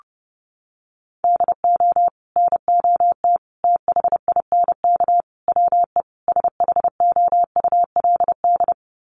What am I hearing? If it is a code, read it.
Morse code, heard as a keyed tone, about 30 wpm, 700 Hz. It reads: DO NOT THINK WE SHOULD
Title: Sets of 5 Words Encore 30wpm